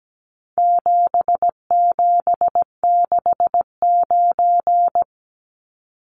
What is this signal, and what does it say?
Morse code, heard as a keyed tone, about 17 wpm, 700 Hz. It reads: 7769
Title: Numbers - 4-Digit 17wpm